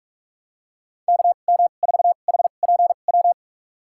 Morse code, heard as a keyed tone, about 46 wpm, spaced as for 33 wpm, 690 Hz. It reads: KM4HPW
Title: USA Call Signs - 6 Characters - 2x Character Spacing - 45wpm